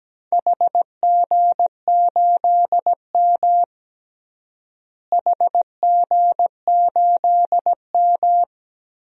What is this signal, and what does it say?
Morse code, heard as a keyed tone, about 17 wpm, 700 Hz. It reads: HG8M HG8M
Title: DXCC-100 Call Signs - 4 Characters - Repeated 3x - 17wpm